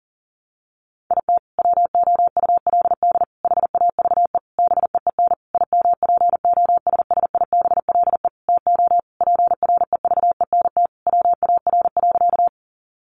Morse code, heard as a keyed tone, about 40 wpm, 700 Hz. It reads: IT WOULD HAVE BEEN IMPOSSIBLE TO PREVENT WAR.
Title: Sentences from Top 1100 Words 40wpm